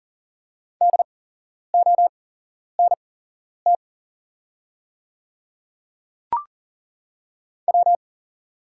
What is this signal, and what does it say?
Morse code, heard as a keyed tone, about 40 wpm, spaced as for 11 wpm, 700 Hz. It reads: DONT W
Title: 8x Character Spacing - Top 500 Words - 40wpm